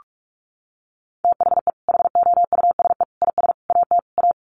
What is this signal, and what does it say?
Morse code, heard as a keyed tone, about 44 wpm, 700 Hz. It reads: THE HOUSE IS AT A
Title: Sentences from Top 1500 Words 45wpm